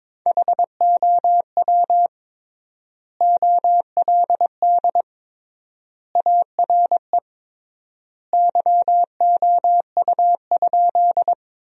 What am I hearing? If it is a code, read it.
Morse code, heard as a keyed tone, about 22 wpm, 700 Hz. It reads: HOW OLD ARE YOU?
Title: 3x Word-Spacing - Sentences from Top 200 Words - 22wpm